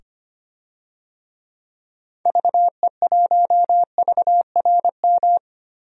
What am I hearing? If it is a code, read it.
Morse code, heard as a keyed tone, about 25 wpm, 700 Hz. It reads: VE1VRM
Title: DXCC-100 Call Signs - 6 Characters - 25wpm